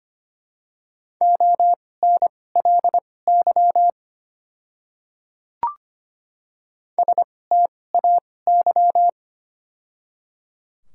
Morse code, heard as a keyed tone, about 25 wpm, spaced as for 18 wpm, 700 Hz. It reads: ONLY STAY
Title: 2x Character Spacing - Top 500 Words - 25wpm